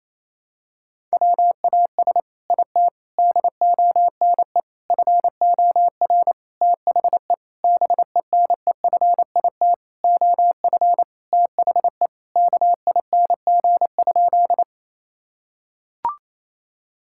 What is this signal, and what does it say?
Morse code, heard as a keyed tone, about 28 wpm, 700 Hz. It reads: WAS IT DONE FOR THE BENEFIT OF THE KING?
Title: Sentences from Top 1000 Words - Review Part 6 - 28wpm